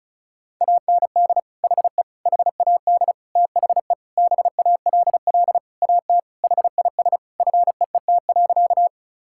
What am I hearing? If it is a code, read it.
Morse code, heard as a keyed tone, about 36 wpm, 690 Hz. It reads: AND HE HAD THE BALL AT HIS FEET.
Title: Sentences from Top 1000 Words - Review Part 6 - 35wpm